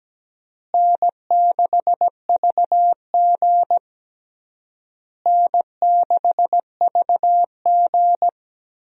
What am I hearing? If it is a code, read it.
Morse code, heard as a keyed tone, about 17 wpm, 700 Hz. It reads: N6VG N6VG
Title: USA Call Signs - 4 Characters - Repeated 3x - 17wpm